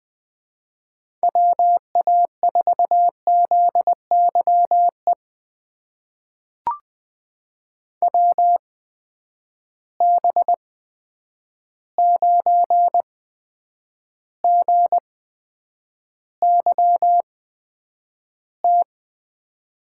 Morse code, heard as a keyed tone, about 20 wpm, spaced as for 9 wpm, 700 Hz. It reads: WA4ZYE WB9GYT
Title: USA Call Signs - 6 Characters - 8x Character Spacing - 20wpm